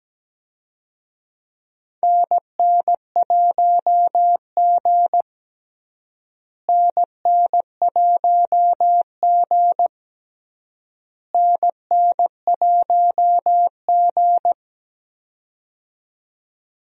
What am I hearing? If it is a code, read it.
Morse code, heard as a keyed tone, about 17 wpm, 700 Hz. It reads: NN1G NN1G NN1G
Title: USA Call Signs - 4 Characters - Repeated 3x - 17wpm